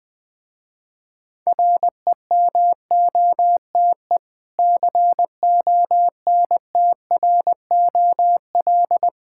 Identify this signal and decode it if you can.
Morse code, heard as a keyed tone, about 20 wpm, 700 Hz. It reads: REMOTE CONTROL
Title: Difficult Words with Hints - 20wpm